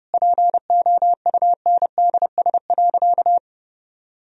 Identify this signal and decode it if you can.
Morse code, heard as a keyed tone, about 32 wpm, 700 Hz. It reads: POUNDS.
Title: Sentences from Top 1000 Words - Review Part 4 - 30wpm